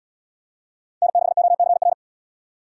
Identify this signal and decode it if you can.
Morse code, heard as a keyed tone, about 75 wpm, 700 Hz. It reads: N6OCR